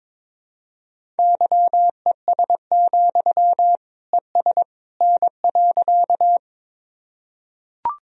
Morse code, heard as a keyed tone, about 21 wpm, 700 Hz. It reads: YES, ES N.
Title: Sentences from Top 500 Words and Abbreviations 22wpm